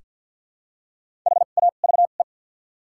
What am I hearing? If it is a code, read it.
Morse code, heard as a keyed tone, about 54 wpm, spaced as for 37 wpm, 700 Hz. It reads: HAVE
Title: 2x Character Spacing - Top 200 Words - 50wpm